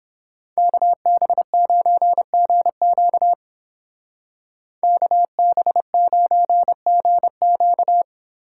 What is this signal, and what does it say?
Morse code, heard as a keyed tone, about 29 wpm, 700 Hz. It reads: KB9GQ KB9GQ